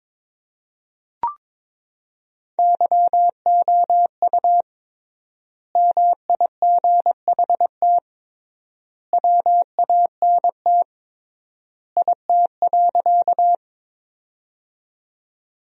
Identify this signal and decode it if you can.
Morse code, heard as a keyed tone, about 22 wpm, 700 Hz. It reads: YOU MIGHT WANT IT.
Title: 3x Word-Spacing - Sentences from Top 300 Words - 22wpm